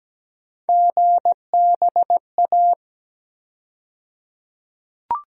Morse code, heard as a keyed tone, about 17 wpm, 700 Hz. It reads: GBA
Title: USA Call Sign Postfixes - 3 Letters 17wpm